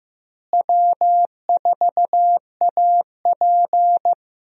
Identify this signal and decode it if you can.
Morse code, heard as a keyed tone, about 15 wpm, 700 Hz. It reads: W4AP
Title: USA Call Signs - 4 Characters - Repeated 3x - 15wpm